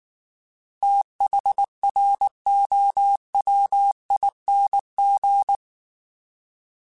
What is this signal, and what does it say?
Morse code, heard as a keyed tone, about 19 wpm, 800 Hz. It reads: THROWING
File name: Fast TH @18